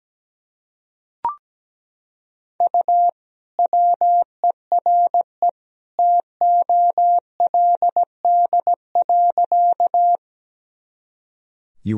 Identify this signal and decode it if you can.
Morse code, heard as a keyed tone, about 17 wpm, 700 Hz. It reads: U WERE TOLD.
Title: Sentences from Top 400 Words and Abbreviations 17wpm